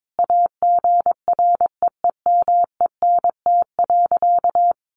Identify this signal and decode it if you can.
Morse code, heard as a keyed tone, about 22 wpm, 700 Hz. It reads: AGREEMENT.
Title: Sentences from Top 1500 Words 22wpm